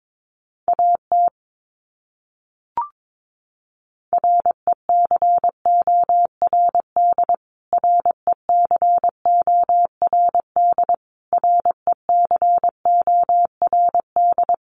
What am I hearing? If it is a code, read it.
Morse code, heard as a keyed tone, about 22 wpm, 700 Hz. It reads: AT RECORD RECORD RECORD
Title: Top 500 Words Repeated 22wpm